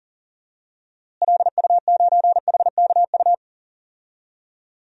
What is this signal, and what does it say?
Morse code, heard as a keyed tone, about 40 wpm, 700 Hz. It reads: LU9HKU
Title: DXCC-100 Call Signs - 6 Characters - Repeated 3x - 40wpm